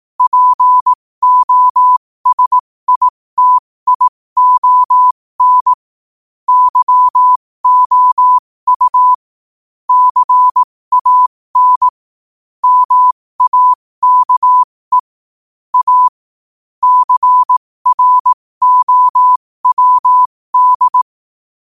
Morse code, heard as a keyed tone, about 18 wpm, spaced as for 15 wpm, 1000 Hz. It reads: POSITION YOU CAN MAKE A CROWD
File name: qotdfiles / 15wpm